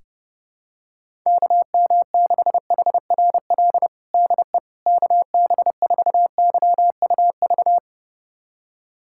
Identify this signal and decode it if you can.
Morse code, heard as a keyed tone, about 32 wpm, 700 Hz. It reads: KM6HRL DE KB4YUV
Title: USA Callsign DE Callsign 30wpm